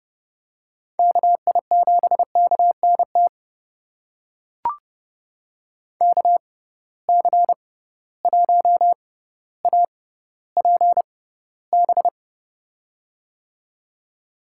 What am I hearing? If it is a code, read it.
Morse code, heard as a keyed tone, about 32 wpm, spaced as for 16 wpm, 700 Hz. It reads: KI7KNT KC1APB
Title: USA Call Signs - 6 Characters - 6x Character Spacing - 30wpm